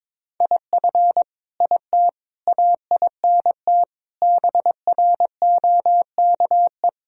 Morse code, heard as a keyed tone, about 22 wpm, 700 Hz. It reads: IF IT AINT BROKE
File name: Sentences with 7-Word Phrases - 22wpm